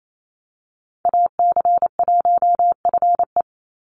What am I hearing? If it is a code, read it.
Morse code, heard as a keyed tone, about 28 wpm, 700 Hz. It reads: AC1FE
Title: Call Signs - 28wpm